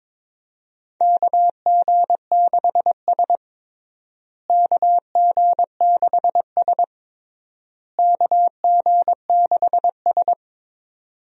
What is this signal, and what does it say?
Morse code, heard as a keyed tone, about 22 wpm, 700 Hz. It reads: KG6S KG6S KG6S
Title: USA Call Signs - 4 Characters - Repeated 3x - 22wpm